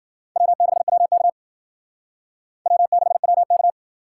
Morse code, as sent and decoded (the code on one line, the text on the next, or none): .-- -.... .-- -.- / .-- -.... .-- -.-
W6WK W6WK